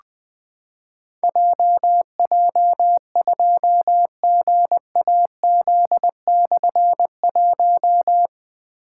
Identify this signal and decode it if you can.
Morse code, heard as a keyed tone, about 20 wpm, 700 Hz. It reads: JJ2GAZ/1
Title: Difficult Call Signs 20wpm